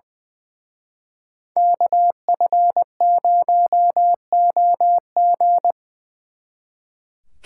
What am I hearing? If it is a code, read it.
Morse code, heard as a keyed tone, about 24 wpm, 700 Hz. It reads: KF0OG
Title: Speed Racing - USA 2x2 Call Signs 20wpm